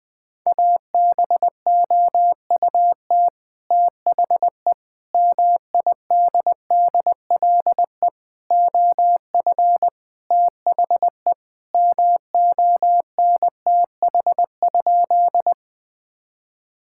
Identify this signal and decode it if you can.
Morse code, heard as a keyed tone, about 19 wpm, 700 Hz. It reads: ABOUT THE MIDDLE OF THE MONTH?
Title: Sentences from Top 1000 Words - Review Part 6 - 20wpm